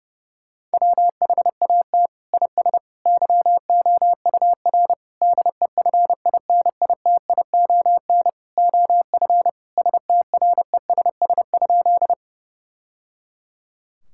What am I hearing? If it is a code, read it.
Morse code, heard as a keyed tone, about 30 wpm, 700 Hz. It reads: WHAT IS YOUR DEFINITION OF STRESS?
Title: Sentences from Top 1000 Words - Review Part 6 - 30wpm